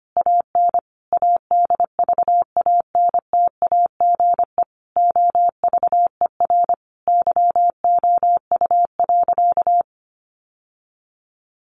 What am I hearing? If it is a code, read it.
Morse code, heard as a keyed tone, about 25 wpm, 700 Hz. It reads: AN ADVANTAGE OVER YOU.
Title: Sentences from Top 900 Words 25wpm